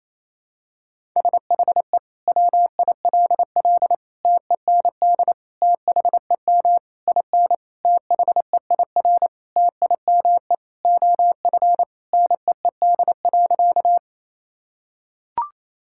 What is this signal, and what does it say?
Morse code, heard as a keyed tone, about 28 wpm, 700 Hz. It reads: SHE WILL TEND THEM IN THEIR TIME OF NEED.
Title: Sentences from Top 1000 Words - Review Part 5 - 28wpm